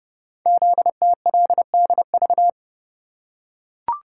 Morse code, heard as a keyed tone, about 28 wpm, 700 Hz. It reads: ZTLDV
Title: Code Groups - 5 Letters - 30wpm